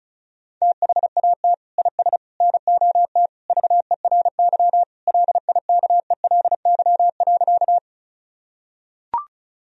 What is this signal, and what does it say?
Morse code, heard as a keyed tone, about 36 wpm, 700 Hz. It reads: THAT IS NOT VERY LIKELY.
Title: Sentences from Top 1000 Words - Review Part 4 - 35wpm